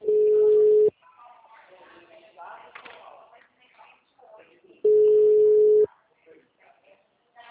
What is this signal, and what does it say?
Ringback tone (Europe, 425 Hz), with speech.
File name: out- -2055- - - .53505